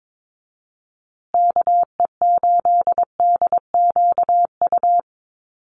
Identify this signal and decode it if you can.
Morse code, heard as a keyed tone, about 22 wpm, 700 Hz. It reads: KE8DQU
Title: USA Call Signs Repeated 3x 22wpm